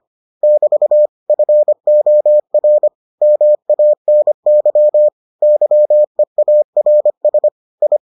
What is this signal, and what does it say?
Morse code, heard as a keyed tone, about 25 wpm, 600 Hz. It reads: <BT> FOR MANY YEARS I